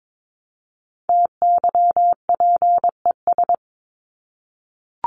Morse code, heard as a keyed tone, about 22 wpm, 700 Hz. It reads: TYPES
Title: ES:S Suffix - Top 100 Words with Base Words 22wpm